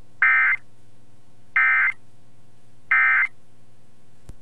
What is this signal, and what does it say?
Emergency Alert System (SAME) tones, heard as harsh buzzing bursts. It is heard from 0.2 to 3.3 s.